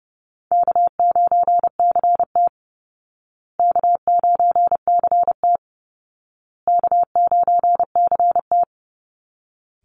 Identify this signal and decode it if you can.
Morse code, heard as a keyed tone, about 30 wpm, 700 Hz. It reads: K9CT K9CT K9CT
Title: Easy Call Signs Repeated 3x - 30wpm